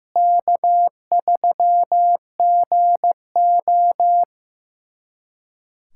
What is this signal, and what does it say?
Morse code, heard as a keyed tone, about 15 wpm, 700 Hz. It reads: K3GO
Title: USA Callsign DE Callsign - 4 Characters - 15wpm